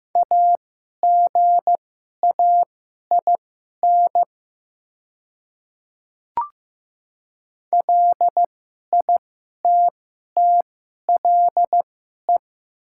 Morse code, heard as a keyed tone, about 15 wpm, spaced as for 11 wpm, 700 Hz. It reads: AGAIN LITTLE
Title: 2x Character Spacing - Top 200 Words - 15wpm